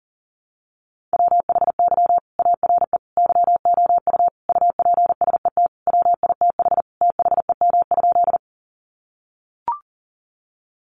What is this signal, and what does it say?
Morse code, heard as a keyed tone, about 40 wpm, 700 Hz. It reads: WHY ARE YOU UPSET WITH THEM?
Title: Sentences from Top 800 Words 40wpm